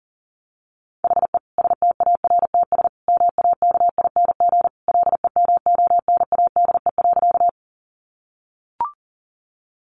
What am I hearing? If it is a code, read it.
Morse code, heard as a keyed tone, about 40 wpm, 700 Hz. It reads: HE STARTS MAKING LEMONADE.
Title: Additional Easy Phrases 40wpm